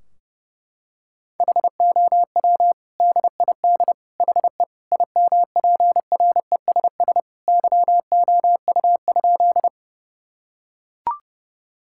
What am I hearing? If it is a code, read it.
Morse code, heard as a keyed tone, about 30 wpm, 700 Hz. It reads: HOW DID HE IMPRESS YOU?
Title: Sentences from Top 1600 Words 30wpm